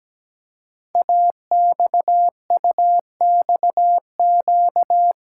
Morse code, heard as a keyed tone, about 17 wpm, 700 Hz. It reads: AXUXQ
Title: Code Groups - 5 Letters - 17wpm